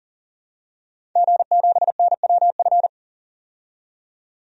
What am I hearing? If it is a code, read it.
Morse code, heard as a keyed tone, about 40 wpm, 700 Hz. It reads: G7NWF
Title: DXCC-100 Call Signs - 5 Characters - 40wpm